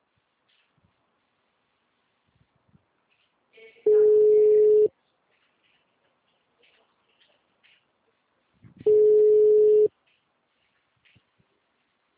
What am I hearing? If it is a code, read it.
Ringback tone (Europe, 425 Hz).
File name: out- -2008- - - .121